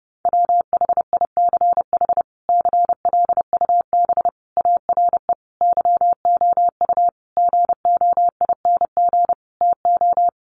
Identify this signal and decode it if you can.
Morse code, heard as a keyed tone, about 30 wpm, 700 Hz. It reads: WHICH CLUB ARE YOU GOING TO
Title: Sentences from Top 1200 Words 30wpm